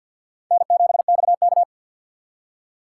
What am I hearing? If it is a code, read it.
Morse code, heard as a keyed tone, about 46 wpm, 700 Hz. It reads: N7XK
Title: USA Call Signs - 4 Characters - 2x Character Spacing - 50wpm